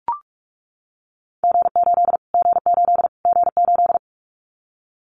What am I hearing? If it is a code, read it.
Morse code, heard as a keyed tone, about 44 wpm, 700 Hz. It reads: G8 G8 G8
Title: Callsign Prefixes for Top 25 DXCC - Repeated 3x - 45wpm